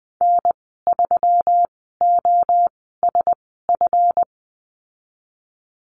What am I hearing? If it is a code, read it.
Morse code, heard as a keyed tone, about 20 wpm, spaced as for 14 wpm, 700 Hz. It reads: N3OSF
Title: 140 Lesson 23g - Letter f - Call Signs 20wpm